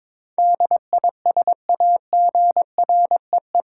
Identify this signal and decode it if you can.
Morse code, heard as a keyed tone, about 22 wpm, 700 Hz. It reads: DISAGREE